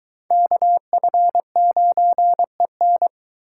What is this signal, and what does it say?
Morse code, heard as a keyed tone, about 23 wpm, 700 Hz. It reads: KF9EN